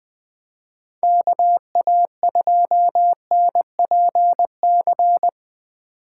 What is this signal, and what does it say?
Morse code, heard as a keyed tone, about 20 wpm, 700 Hz. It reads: KA2NPC
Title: USA Call Signs - 6 Characters - 2x Character Spacing - 20wpm